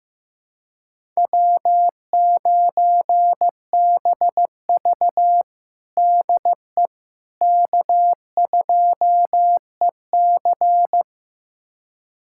Morse code, heard as a keyed tone, about 15 wpm, 700 Hz. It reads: W9BV DE K2EC
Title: USA Callsign DE Callsign - 4 Characters - 15wpm